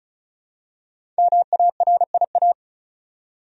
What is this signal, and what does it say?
Morse code, heard as a keyed tone, about 36 wpm, 700 Hz. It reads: MARIA
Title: USA Common Names - 35wpm